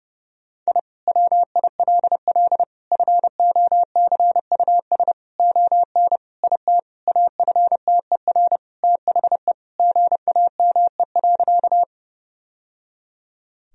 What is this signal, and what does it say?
Morse code, heard as a keyed tone, about 32 wpm, 700 Hz. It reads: I WILL FOCUS ON IT AFTER THE GAME.
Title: Sentences from Top 1000 Words - Review Part 4 - 30wpm